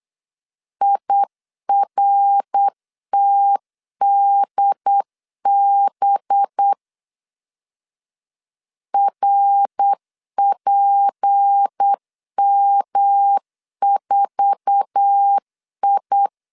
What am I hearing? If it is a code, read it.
Morse code, heard as a keyed tone, about 8 wpm, 800 Hz. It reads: IRTDB RPM4I
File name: LEZIONI / 8 YJ45 / 8 YJ4 8wpm 3246